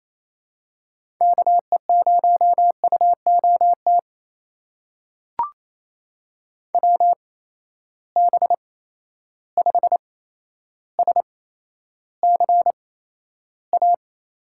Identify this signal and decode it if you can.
Morse code, heard as a keyed tone, about 28 wpm, spaced as for 12 wpm, 700 Hz. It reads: KE0UOT WB5SCA
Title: USA Call Signs - 6 Characters - 8x Character Spacing - 28wpm